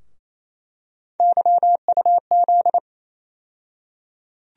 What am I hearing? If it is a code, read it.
Morse code, heard as a keyed tone, about 28 wpm, 700 Hz. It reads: YUZ
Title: USA Call Sign Postfixes - 3 Letters 28wpm